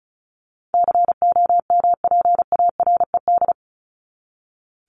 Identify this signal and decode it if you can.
Morse code, heard as a keyed tone, about 34 wpm, 700 Hz. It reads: COMPARED
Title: ED Suffix - Top 100 Words with Base Words 35wpm